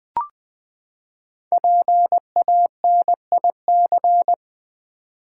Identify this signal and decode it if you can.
Morse code, heard as a keyed tone, about 20 wpm, 700 Hz. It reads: PANIC
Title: Five Letter Words - 20wpm